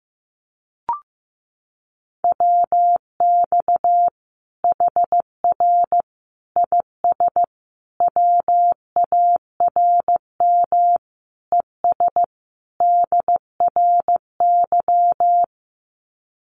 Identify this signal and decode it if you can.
Morse code, heard as a keyed tone, about 14 wpm, 700 Hz. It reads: WX HR IS WARM ES DRY
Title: Common QSO Phrases 15wpm